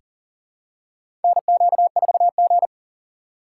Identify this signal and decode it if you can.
Morse code, heard as a keyed tone, about 40 wpm, 700 Hz. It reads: NQ4G
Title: USA Call Signs - 4 Characters - 3x Character Spacing - 40wpm